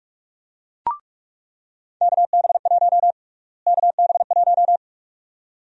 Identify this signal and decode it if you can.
Morse code, heard as a keyed tone, about 42 wpm, 700 Hz. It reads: KB1 KB1
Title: USA Call Sign Prefixes - Repeated 2x - 45wpm